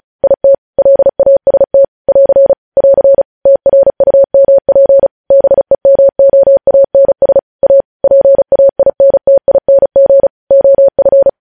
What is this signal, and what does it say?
Morse code, heard as a keyed tone, about 36 wpm, 550 Hz. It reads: IT LAST <AR> <AR> TRUMP BEMOANS A PAINTING OF
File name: podcast / News headlines-1x-35x35